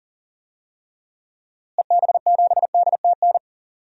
Morse code, heard as a keyed tone, about 40 wpm, 700 Hz. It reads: EB7DTN
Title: DXCC-100 Call Signs - 6 Characters - Repeated 2x - 40wpm